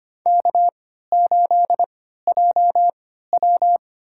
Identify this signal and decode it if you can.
Morse code, heard as a keyed tone, about 25 wpm, spaced as for 14 wpm, 700 Hz. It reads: K8JW
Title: USA Call Signs - 4 Characters - 3x Character Spacing - 25wpm